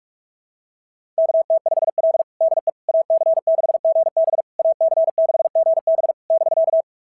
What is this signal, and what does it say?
Morse code, heard as a keyed tone, about 44 wpm, 690 Hz. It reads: KT5L DE AC6GB AC6GB <BK>
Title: SOTA QSO Practice 45wpm